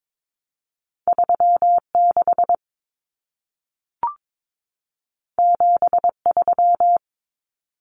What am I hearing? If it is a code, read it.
Morse code, heard as a keyed tone, about 22 wpm, 700 Hz. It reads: 36 73
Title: Common Cut Numbers - 2-Digit 22wpm